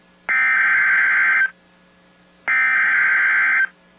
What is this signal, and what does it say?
Emergency Alert System (SAME) tones (header bursts), heard as harsh buzzing bursts. It is heard from 0.3 to 3.6 s.